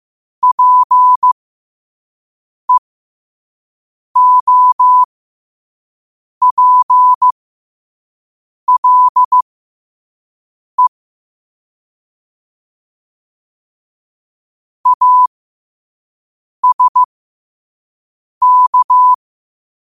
Morse code, heard as a keyed tone, about 15 wpm, spaced as for 5 wpm, 1000 Hz. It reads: PEOPLE ASK
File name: 5wpm